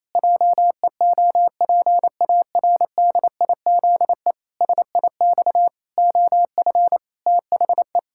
Morse code, heard as a keyed tone, about 28 wpm, 700 Hz. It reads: JEOPARDIZE SIX OF THE
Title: Pangram Challenge v2 - 28wpm